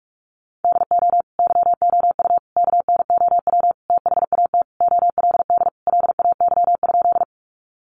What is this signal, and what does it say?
Morse code, heard as a keyed tone, about 44 wpm, 700 Hz. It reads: DO YOU KNOW THAT OLD LAY?
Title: Sentences from Top 400 Words 45wpm